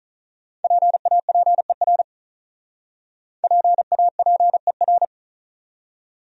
Morse code, heard as a keyed tone, about 46 wpm, 700 Hz. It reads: PAPER PAPER